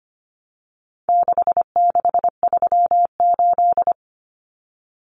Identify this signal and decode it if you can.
Morse code, heard as a keyed tone, about 25 wpm, 700 Hz. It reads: 6638
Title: Numbers - 4-Digit 25wpm